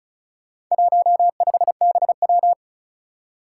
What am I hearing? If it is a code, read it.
Morse code, heard as a keyed tone, about 36 wpm, 700 Hz. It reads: 15BW
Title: Four Challenging Letter-Number Combinations 35wpm